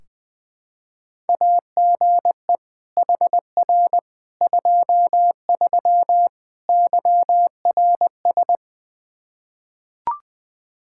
Morse code, heard as a keyed tone, about 20 wpm, 700 Hz. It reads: AGE HR 23 YRS
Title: Practice QSO - Exchange 3 - More Info - 20wpm